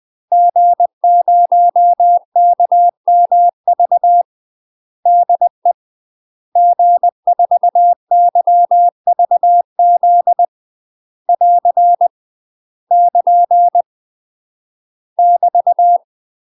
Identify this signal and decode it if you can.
Morse code, heard as a keyed tone, about 20 wpm, 700 Hz. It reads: G0KMV DE G4YVZ <AR> <KN> <BT>
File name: UK MASTER QSOs TRAINING / 20WPM / UK MASTER QSOs 0037